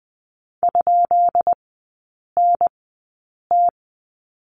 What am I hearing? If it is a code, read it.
Morse code, heard as a keyed tone, about 20 wpm, spaced as for 8 wpm, 690 Hz. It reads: ?NT
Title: 159 Lesson 26e - Letter g - Three Characters 20wpm